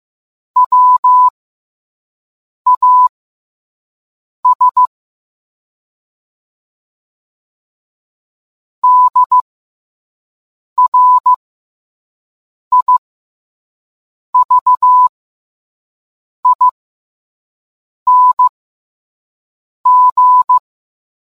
Morse code, heard as a keyed tone, about 15 wpm, spaced as for 5 wpm, 1000 Hz. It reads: WAS DRIVING